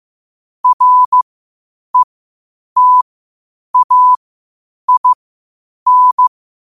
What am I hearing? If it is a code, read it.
Morse code, heard as a keyed tone, about 15 wpm, spaced as for 8 wpm, 1000 Hz. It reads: RETAIN
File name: qotdfiles / 8wpm